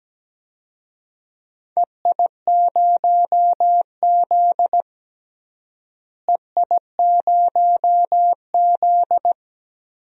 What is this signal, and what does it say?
Morse code, heard as a keyed tone, about 17 wpm, 700 Hz. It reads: EI0Z EI0Z
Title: DXCC-100 Call Signs - 4 Characters - Repeated 2x - 17wpm